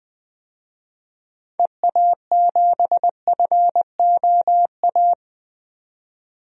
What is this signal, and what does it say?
Morse code, heard as a keyed tone, about 20 wpm, 700 Hz. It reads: EA7FOA
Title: DXCC-100 Call Signs - 6 Characters - 20wpm